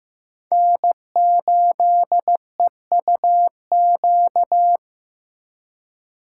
Morse code, heard as a keyed tone, about 15 wpm, 700 Hz. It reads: N8EUQ
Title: USA Call Signs - 5 Characters - 3x Character Spacing - 15wpm